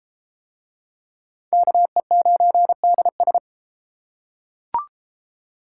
Morse code, heard as a keyed tone, about 31 wpm, 700 Hz. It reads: KE9DS